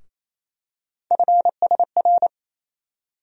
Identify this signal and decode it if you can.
Morse code, heard as a keyed tone, about 27 wpm, 700 Hz. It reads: FSR
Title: USA Call Sign Postfixes - 3 Letters 28wpm